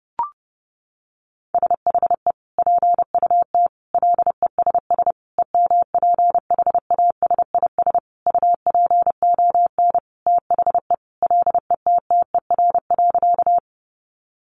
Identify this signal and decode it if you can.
Morse code, heard as a keyed tone, about 30 wpm, 700 Hz. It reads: SHE PUT LESS EMPHASIS UPON THE LETTER.
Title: Sentences from Top 1400 Words 30wpm